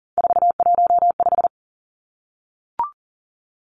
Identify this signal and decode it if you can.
Morse code, heard as a keyed tone, about 40 wpm, 700 Hz. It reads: 415
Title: Numbers - 3-Digit 40wpm